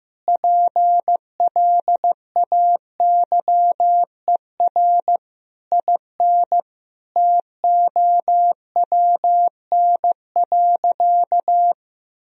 Morse code, heard as a keyed tone, about 15 wpm, 700 Hz. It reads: PLAYER IN TOWN.
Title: Sentences from Top 1000 Words - Review Part 4 - 15wpm